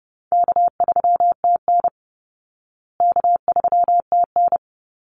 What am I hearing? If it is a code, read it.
Morse code, heard as a keyed tone, about 30 wpm, 700 Hz. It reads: K3TN K3TN
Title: Easy Call Signs Repeated 2x - 30wpm